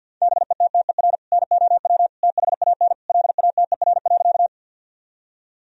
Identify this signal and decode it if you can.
Morse code, heard as a keyed tone, about 50 wpm, 700 Hz. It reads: BETTER NOW THAN LATER.
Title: Sentences from Top 1000 Words - Review Part 4 - 50wpm